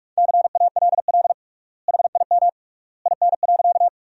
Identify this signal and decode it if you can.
Morse code, heard as a keyed tone, about 41 wpm, 700 Hz. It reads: CALL HIM IN.
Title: 3x Word-Spacing - Sentences from Top 100 Words - 45wpm